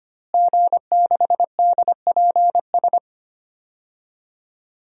Morse code, heard as a keyed tone, about 25 wpm, 700 Hz. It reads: G6DPS
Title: DXCC-100 Call Signs - 5 Characters - Repeated 2x - 25wpm